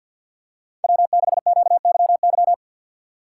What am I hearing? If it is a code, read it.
Morse code, heard as a keyed tone, about 53 wpm, 700 Hz. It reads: W6QYY
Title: USA Call Signs - 4x Character Spacing - 50wpm